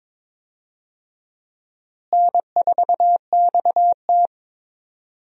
Morse code, heard as a keyed tone, about 22 wpm, 700 Hz. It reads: N4XT
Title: USA Call Signs - 4 Characters - Repeated 2x - 22wpm